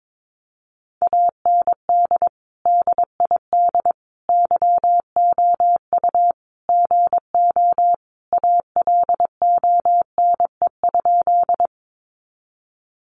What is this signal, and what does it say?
Morse code, heard as a keyed tone, about 22 wpm, 700 Hz. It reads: AND DID YOU GO ALONE?
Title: Sentences from Top 1000 Words 22wpm